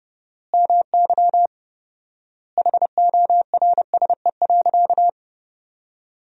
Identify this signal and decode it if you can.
Morse code, heard as a keyed tone, about 30 wpm, 700 Hz. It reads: MY HORSE.
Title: 4x Word-Spacing - Sentences from Top 500 Words - 30wpm